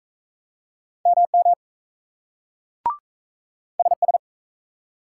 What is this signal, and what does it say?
Morse code, heard as a keyed tone, about 42 wpm, 700 Hz. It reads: MM SS